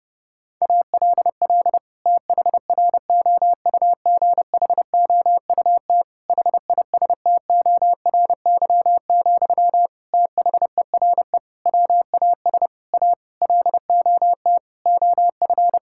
Morse code, heard as a keyed tone, about 30 wpm, 700 Hz. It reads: ALL THROUGHOUT HISTORY, THERE WAS A LOT OF
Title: Sentences from Top 1000 Words - Review Part 5 - 30wpm